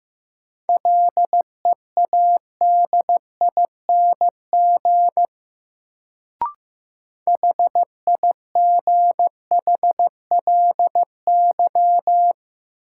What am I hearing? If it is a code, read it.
Morse code, heard as a keyed tone, about 15 wpm, 700 Hz. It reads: LEADING HIGHLY
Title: Sentences from Top 1100 Words - New Word Drill Alt - 15wpm